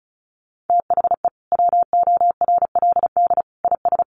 Morse code, heard as a keyed tone, about 34 wpm, 700 Hz. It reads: THE WORLD IS
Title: Sentences from Top 600 Words 35wpm